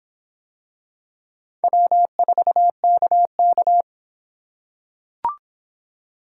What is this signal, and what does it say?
Morse code, heard as a keyed tone, about 25 wpm, 700 Hz. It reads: W4KK